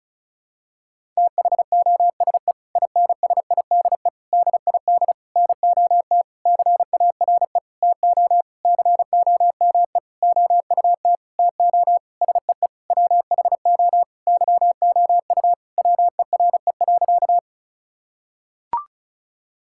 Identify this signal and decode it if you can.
Morse code, heard as a keyed tone, about 34 wpm, 700 Hz. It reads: THOSE INSIDE DID NOT CARE TO COME OUT TO SEE WHO YOU WERE.
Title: Sentences from Top 1000 Words - Review Part 6 - 35wpm